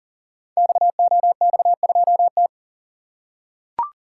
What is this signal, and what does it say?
Morse code, heard as a keyed tone, about 39 wpm, 690 Hz. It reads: XOX2T
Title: Code Groups - 5 Letter:Numbers - 40wpm